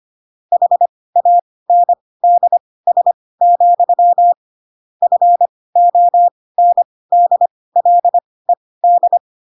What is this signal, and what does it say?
Morse code, heard as a keyed tone, about 25 wpm, spaced as for 18 wpm, 700 Hz. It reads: HANDS, FONDLED
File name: Pinocchio / Pinocchio ch5 12 25 ik1hge